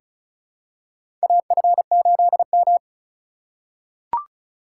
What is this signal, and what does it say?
Morse code, heard as a keyed tone, about 36 wpm, 700 Hz. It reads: AF8M
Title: USA Call Signs - 4 Characters - 2x Character Spacing - 35wpm